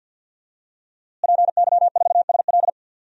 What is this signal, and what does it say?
Morse code, heard as a keyed tone, about 52 wpm, 700 Hz. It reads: PY4SL
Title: DXCC-100 Call Signs - 5 Characters - Repeated 3x - 50wpm